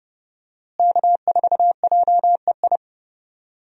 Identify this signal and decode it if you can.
Morse code, heard as a keyed tone, about 32 wpm, 700 Hz. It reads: K4JEI
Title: USA Call Signs - 5 Characters - Repeated 2x - 30wpm